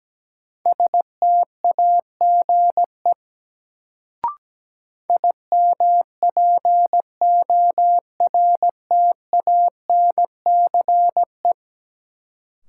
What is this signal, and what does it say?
Morse code, heard as a keyed tone, about 17 wpm, 700 Hz. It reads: STAGE IMPORTANCE
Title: Sentences from Top 1200 Words - New Word Drill - 17wpm